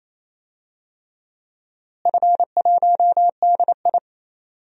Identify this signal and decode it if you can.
Morse code, heard as a keyed tone, about 28 wpm, 700 Hz. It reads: F1DI
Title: DXCC-100 Call Signs - 4 Characters - Repeated 3x - 28wpm